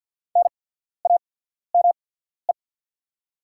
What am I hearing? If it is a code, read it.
Morse code, heard as a keyed tone, about 50 wpm, spaced as for 14 wpm, 690 Hz. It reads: NAME
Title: 8x Character Spacing - Top 200 Words - 50wpm